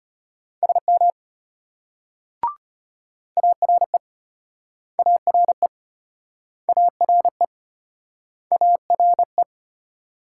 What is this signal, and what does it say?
Morse code, heard as a keyed tone, about 36 wpm, 700 Hz. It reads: SM ARE ARE ARE ARE